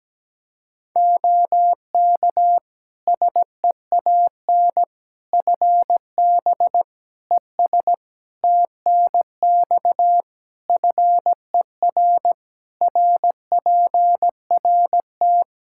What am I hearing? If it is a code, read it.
Morse code, heard as a keyed tone, about 17 wpm, 700 Hz. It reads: OK SEAN FB ES TNX FER RPRT
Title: Practice QSO - Exchange 2 - Info - 17wpm